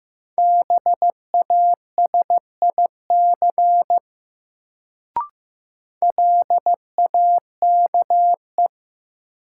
Morse code, heard as a keyed tone, about 16 wpm, 700 Hz. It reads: BASIC LAKE
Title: Sentences from Top 1000 Words - Review Part 3 - New Word Drill - 15wpm